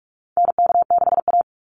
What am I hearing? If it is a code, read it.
Morse code, heard as a keyed tone, about 44 wpm, 700 Hz. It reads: NK6A
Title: NA SOTA Chaser Callsigns 45wpm